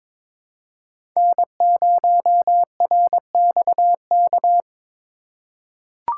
Morse code, heard as a keyed tone, about 22 wpm, 700 Hz. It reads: N0RXK
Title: USA Call Signs - 5 Characters - 4x Character Spacing - 22wpm